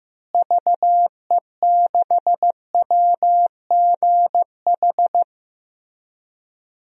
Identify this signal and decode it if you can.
Morse code, heard as a keyed tone, about 15 wpm, 700 Hz. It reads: VE6WGH
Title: DXCC-100 Call Signs - 6 Characters - Repeated 2x - 15wpm